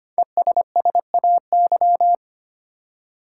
Morse code, heard as a keyed tone, about 25 wpm, 700 Hz. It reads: ESSAY
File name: 5 Letter Words - First 3 Letters Given - 25wpm